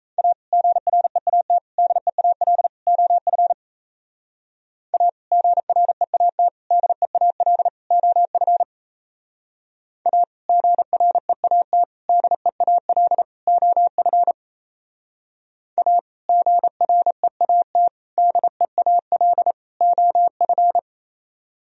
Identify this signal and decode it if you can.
Morse code, heard as a keyed tone, about 35 wpm, 700 Hz. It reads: A GREAT DEAL OF A GREAT DEAL OF A GREAT DEAL OF A GREAT DEAL OF